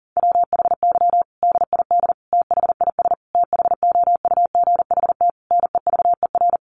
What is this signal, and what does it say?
Morse code, heard as a keyed tone, about 37 wpm, 700 Hz. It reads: WHY DID THIS THOUGHT NEVER
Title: Sentences from Top 900 Words 40wpm